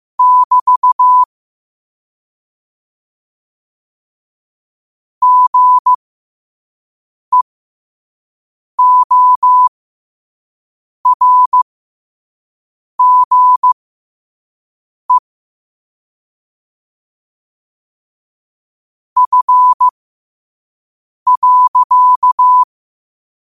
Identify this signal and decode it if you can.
Morse code, heard as a keyed tone, about 15 wpm, spaced as for 5 wpm, 1000 Hz. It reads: <BT> GEORGE F.
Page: New quotes every day in morse code at 5 Words per minute.